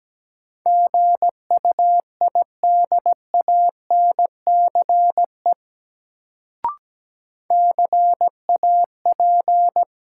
Morse code, heard as a keyed tone, about 17 wpm, 700 Hz. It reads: GUIDANCE CAP
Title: Sentences from Top 1600 Words - New Word Drill Alt - 17wpm